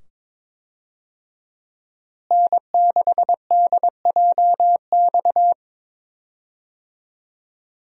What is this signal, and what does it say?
Morse code, heard as a keyed tone, about 22 wpm, 700 Hz. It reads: N6DJX
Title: USA Call Signs - 5 Characters - Repeated 2x - 22wpm